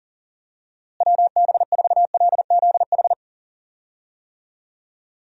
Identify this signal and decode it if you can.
Morse code, heard as a keyed tone, about 37 wpm, 700 Hz. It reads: WB4LZH
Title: USA Call Signs - 6 Characters - Repeated 3x - 40wpm